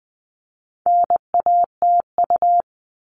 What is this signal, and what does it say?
Morse code, heard as a keyed tone, about 20 wpm, 700 Hz. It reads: NATU
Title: All Cut Numbers - 4-Digit 20wpm